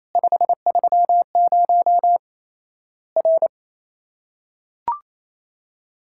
Morse code, heard as a keyed tone, about 28 wpm, 700 Hz. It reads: 530 R
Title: Worked All Europe DX Contest - QTC Exchange - 28wpm